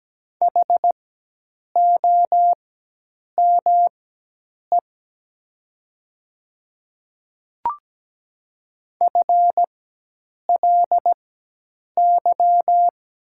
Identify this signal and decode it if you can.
Morse code, heard as a keyed tone, about 17 wpm, spaced as for 8 wpm, 700 Hz. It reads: HOME FLY
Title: 4x Character Spacing - Top 500 Words - 17wpm